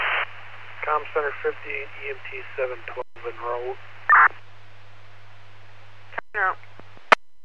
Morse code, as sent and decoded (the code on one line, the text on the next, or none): none